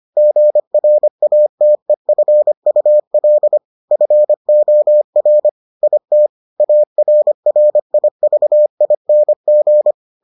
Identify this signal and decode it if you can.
Morse code, heard as a keyed tone, about 25 wpm, 600 Hz. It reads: GRATEFUL FOR IT ARRIVING